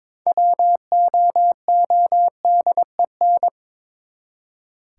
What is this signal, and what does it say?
Morse code, heard as a keyed tone, about 22 wpm, 700 Hz. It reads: WOODEN
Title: Six Letter Words - 22wpm